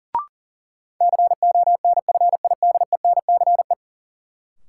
Morse code, heard as a keyed tone, about 40 wpm, 700 Hz. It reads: CONFIDENCE
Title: Sentences from Top 1500 Words - New Word Drill Alt - 40wpm